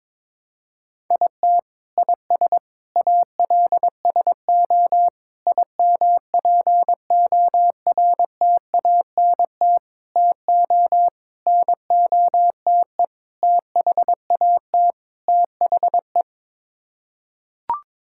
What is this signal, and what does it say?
Morse code, heard as a keyed tone, about 21 wpm, 700 Hz. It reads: IT IS ALSO IMPORTANT TO NOTE THAT THE
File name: practice / 22wpm / Sets of 8 Words - 22wpm